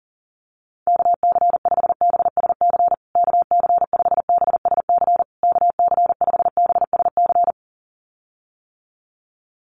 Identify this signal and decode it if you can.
Morse code, heard as a keyed tone, about 40 wpm, 700 Hz. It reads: KC5BSC KC5BSC KC5BSC
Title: USA Call Signs Repeated 3x 40wpm